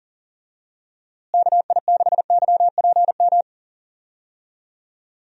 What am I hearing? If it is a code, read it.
Morse code, heard as a keyed tone, about 37 wpm, 700 Hz. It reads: KI6YPM
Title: USA Call Signs - 6 Characters - Repeated 3x - 40wpm